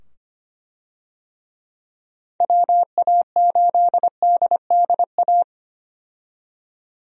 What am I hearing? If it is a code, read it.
Morse code, heard as a keyed tone, about 25 wpm, 700 Hz. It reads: WA8DDA
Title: USA Call Signs - 6 Characters - 25wpm